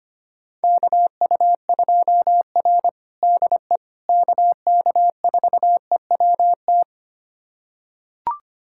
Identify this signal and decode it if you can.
Morse code, heard as a keyed tone, about 25 wpm, 700 Hz. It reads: KU2R DE KK4EWT
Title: USA Callsign DE Callsign 25wpm